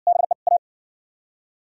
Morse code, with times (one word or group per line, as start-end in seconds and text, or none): 0.1-0.3 BE
0.5-0.6 A